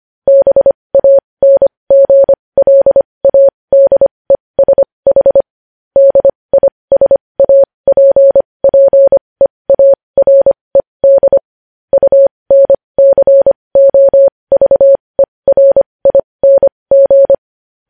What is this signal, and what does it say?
Morse code, heard as a keyed tone, about 25 wpm, spaced as for 20 wpm, 550 Hz. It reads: BANGLADESH DISAPPEARED UNCOVERING
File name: podcast / News headlines-1x-25x20